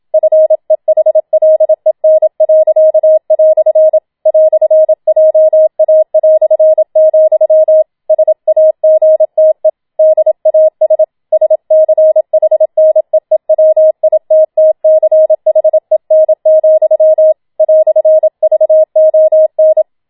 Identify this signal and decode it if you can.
Morse code, heard as a keyed tone, about 27 wpm, 630 Hz. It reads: FEHLEN." "JA", SAGTE DAS SCHNEEWITTCHEN, "VON
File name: Schneewittchen Teil2 27wpm